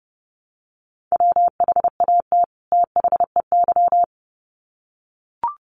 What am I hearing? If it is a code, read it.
Morse code, heard as a keyed tone, about 30 wpm, 700 Hz. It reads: WHAT THEY
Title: Sets of 2 Words 30wpm